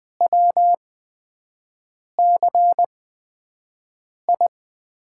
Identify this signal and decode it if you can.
Morse code, heard as a keyed tone, about 20 wpm, spaced as for 5 wpm, 700 Hz. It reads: WCI
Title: USA Call Signs - 8x Character Spacing - 20wpm